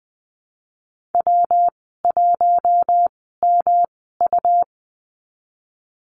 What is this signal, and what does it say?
Morse code, heard as a keyed tone, about 20 wpm, spaced as for 15 wpm, 700 Hz. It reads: W1MU
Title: 128 Lesson 21g - Number 6 - Call Signs 20wpm